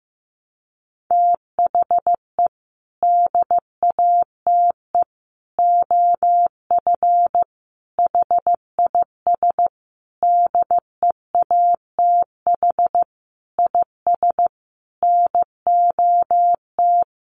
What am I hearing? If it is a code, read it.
Morse code, heard as a keyed tone, about 15 wpm, 700 Hz. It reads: THE DATE OF HIS DEATH IS NOT
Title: Sentences from Top 1100 Words 15wpm